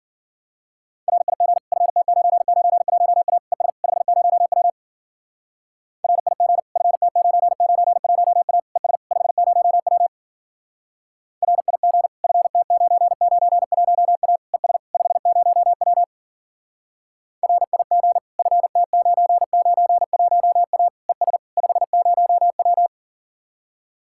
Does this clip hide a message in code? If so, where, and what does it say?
Yes: Morse code from 1.1 to 22.9 s, heard as a keyed tone, about 51 wpm, 700 Hz. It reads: RIG FT991A ES 50W RIG FT991A ES 50W RIG FT991A ES 50W RIG FT991A ES 50W